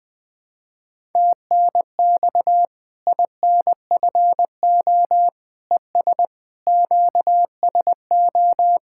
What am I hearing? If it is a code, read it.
Morse code, heard as a keyed tone, about 19 wpm, 700 Hz. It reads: TNX INFO ES QSO
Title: Practice QSO - Exchange 4 - Ending - 20wpm